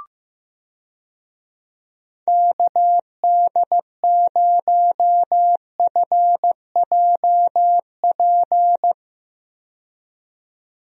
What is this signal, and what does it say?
Morse code, heard as a keyed tone, about 15 wpm, 700 Hz. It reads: KD0FJP
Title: USA Call Signs - 6 Characters - 15wpm